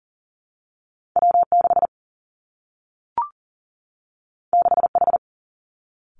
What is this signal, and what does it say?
Morse code, heard as a keyed tone, about 42 wpm, 700 Hz. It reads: W 6 6 H
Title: Two Challenging Letter-Number Combinations 40wpm